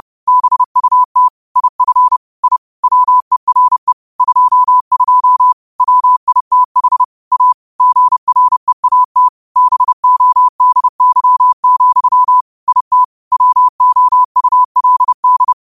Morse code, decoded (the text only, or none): BUT IF I WERE 22 WITH A GREAT BODY, IT WOULD